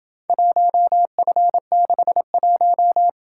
Morse code, heard as a keyed tone, about 27 wpm, 700 Hz. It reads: 1F61